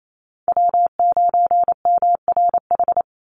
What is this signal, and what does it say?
Morse code, heard as a keyed tone, about 28 wpm, 700 Hz. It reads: W9MRH
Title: NA SOTA Chaser Callsigns 28wpm